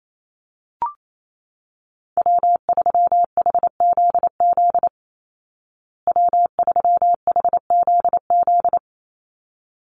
Morse code, heard as a keyed tone, about 28 wpm, 700 Hz. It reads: W3HZZ W3HZZ
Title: Call Signs Repeated 3x - 28wpm